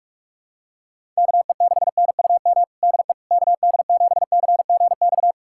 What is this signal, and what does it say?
Morse code, heard as a keyed tone, about 42 wpm, 700 Hz. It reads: KE6NUM DE KD7CGX
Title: USA Callsign DE Callsign - 6 Characters - 45wpm